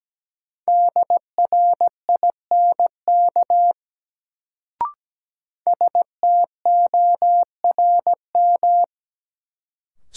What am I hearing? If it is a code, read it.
Morse code, heard as a keyed tone, about 17 wpm, 700 Hz. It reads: DRINK STORM
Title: Sentences from Top 1500 Words - New Word Drill Alt - 17wpm